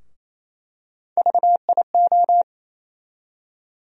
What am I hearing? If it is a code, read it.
Morse code, heard as a keyed tone, about 28 wpm, 700 Hz. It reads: VIO
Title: USA Call Sign Postfixes - 3 Letters 28wpm